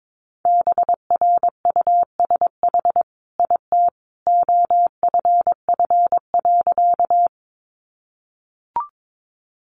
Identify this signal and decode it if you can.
Morse code, heard as a keyed tone, about 22 wpm, 700 Hz. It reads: BRUSH IT OFF.
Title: Sentences from Top 1100 Words 22wpm